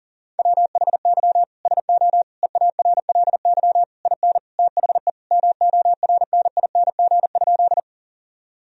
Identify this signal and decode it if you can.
Morse code, heard as a keyed tone, about 37 wpm, 700 Hz. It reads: WHY SO EARLY IN THE MORNING?
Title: Sentences from Top 1000 Words - Review Part 2 - 40wpm